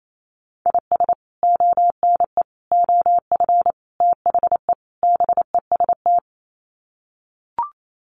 Morse code, heard as a keyed tone, about 28 wpm, 700 Hz. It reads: IS ONE OF THE BEST
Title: Sets of 5 Words 28wpm